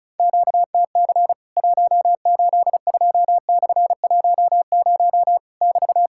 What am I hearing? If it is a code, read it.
Morse code, heard as a keyed tone, about 34 wpm, 700 Hz. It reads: QTC 182/10 <BT>
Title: Worked All Europe DX Contest - QTC Exchange - 35wpm